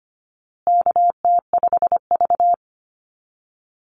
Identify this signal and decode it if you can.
Morse code, heard as a keyed tone, about 25 wpm, 700 Hz. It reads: KT5V
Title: Call Signs Repeated 2x - 25wpm